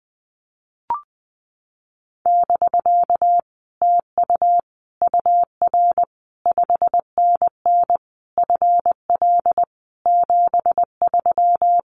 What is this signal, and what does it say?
Morse code, heard as a keyed tone, about 20 wpm, 700 Hz. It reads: <BK> TU UR 5NN FL 73
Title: Common QSO Phrases Encore 20wpm